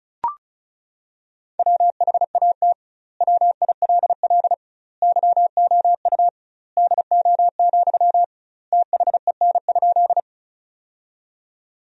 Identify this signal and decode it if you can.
Morse code, heard as a keyed tone, about 34 wpm, 700 Hz. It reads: WHAT WILL YOU DO, THEN?
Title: 2x Word-Spacing - Sentences from Top 100 Words - 35wpm